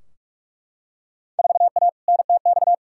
Morse code, heard as a keyed tone, about 43 wpm, 700 Hz. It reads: 4A NTX
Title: Field Day Exchanges - 45wpm